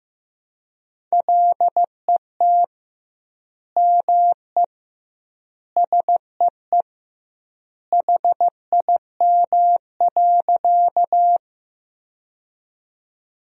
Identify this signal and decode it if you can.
Morse code, heard as a keyed tone, about 15 wpm, 700 Hz. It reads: LET ME SEE HIM.
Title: 2x Word-Spacing - Sentences from Top 300 Words - 15wpm